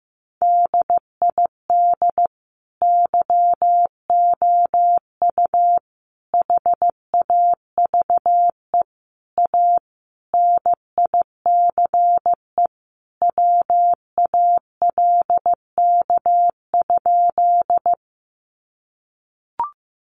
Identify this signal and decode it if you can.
Morse code, heard as a keyed tone, about 15 wpm, 700 Hz. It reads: DID YOU HAVE A NICE WALK?
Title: Sentences from Top 1100 Words 15wpm